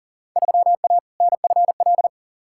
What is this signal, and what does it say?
Morse code, heard as a keyed tone, about 40 wpm, 700 Hz. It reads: 3A NFL
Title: Field Day Exchanges - 40wpm